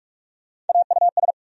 Morse code, heard as a keyed tone, about 46 wpm, 700 Hz. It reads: AUS
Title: USA Call Sign Postfixes - 3 Letters 45wpm